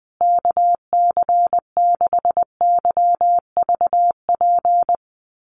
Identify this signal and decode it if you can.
Morse code, heard as a keyed tone, about 20 wpm, 700 Hz. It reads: KC6YVP
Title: USA Call Signs Repeated 3x 20wpm